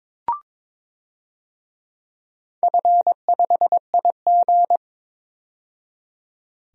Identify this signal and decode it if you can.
Morse code, heard as a keyed tone, about 21 wpm, 700 Hz. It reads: F5IG
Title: DXCC-100 Call Signs - 4 Characters - 22wpm